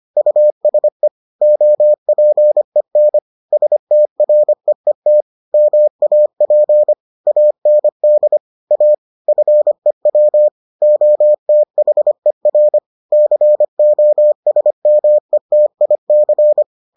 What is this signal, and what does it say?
Morse code, heard as a keyed tone, about 25 wpm, 600 Hz. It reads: USE OPEN STREET MAP AND A FEW OTHER COSMETIC